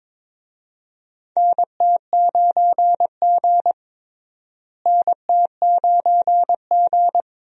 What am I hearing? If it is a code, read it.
Morse code, heard as a keyed tone, about 22 wpm, 700 Hz. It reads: NT9G NT9G
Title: USA Call Signs - 4 Characters - Repeated 2x - 22wpm